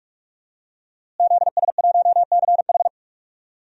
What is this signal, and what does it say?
Morse code, heard as a keyed tone, about 46 wpm, 700 Hz. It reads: ZS1CH
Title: DXCC-100 Call Signs - 5 Characters - Repeated 3x - 45wpm